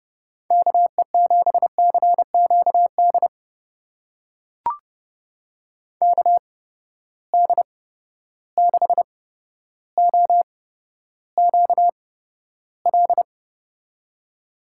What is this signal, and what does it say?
Morse code, heard as a keyed tone, about 31 wpm, spaced as for 13 wpm, 700 Hz. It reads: KE7CQD KD6OQL
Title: USA Call Signs - 6 Characters - 8x Character Spacing - 30wpm